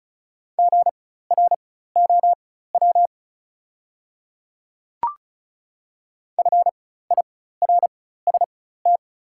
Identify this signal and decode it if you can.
Morse code, heard as a keyed tone, about 36 wpm, spaced as for 16 wpm, 700 Hz. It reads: GROW FIRST
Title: 4x Character Spacing - Top 300 Words - 35wpm